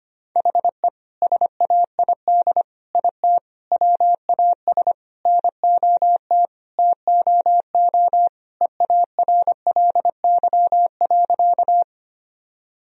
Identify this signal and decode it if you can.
Morse code, heard as a keyed tone, about 25 wpm, 700 Hz. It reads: HE SAID IT WAS NOT TOO EARLY.
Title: Sentences from Top 1000 Words - Review Part 2 - 25wpm